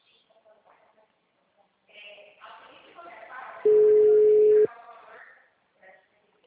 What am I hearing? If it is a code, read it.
Ringback tone (Europe, 425 Hz), with speech.